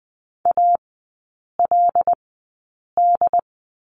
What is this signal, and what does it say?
Morse code, heard as a keyed tone, about 20 wpm, spaced as for 8 wpm, 700 Hz. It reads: ALD
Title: 087 Lesson 15e - Number 5 - Three Characters 20wpm